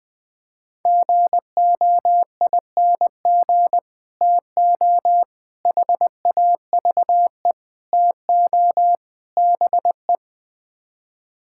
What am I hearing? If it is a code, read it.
Morse code, heard as a keyed tone, about 20 wpm, 700 Hz. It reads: GOING TO HAVE TO BE
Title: Build A Phrase - 5-Word Phrases - 20wpm